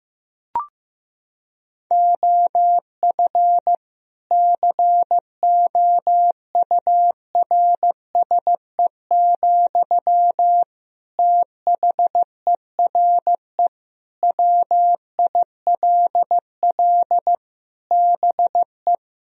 Morse code, heard as a keyed tone, about 15 wpm, 700 Hz. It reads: OF COURSE, THERE WILL BE
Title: Sentences from Top 1000 Words - Review Part 4 - 15wpm